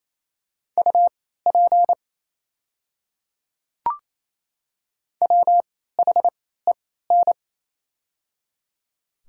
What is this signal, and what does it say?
Morse code, heard as a keyed tone, about 28 wpm, spaced as for 16 wpm, 700 Hz. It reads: UP WHEN
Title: 3x Character Spacing - Top 100 Words - 28wpm